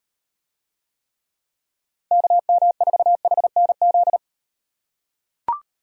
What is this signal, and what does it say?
Morse code, heard as a keyed tone, about 35 wpm, 700 Hz. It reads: KM4HNZ